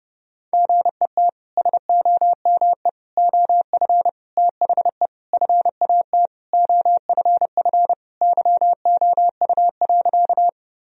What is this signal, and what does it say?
Morse code, heard as a keyed tone, about 30 wpm, 700 Hz. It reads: GET SOME OF THE FAT OFF YOU.
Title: Sentences from Top 1000 Words - Review Part 4 - 30wpm